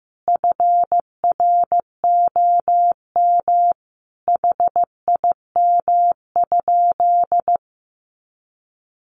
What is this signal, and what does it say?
Morse code, heard as a keyed tone, about 15 wpm, 700 Hz. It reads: FROM HIM?
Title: Sentences from Top 1300 Words 15wpm